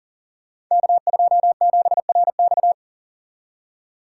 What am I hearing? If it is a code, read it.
Morse code, heard as a keyed tone, about 42 wpm, 700 Hz. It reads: K27RX
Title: Code Groups - 5 Letter:Numbers - 40wpm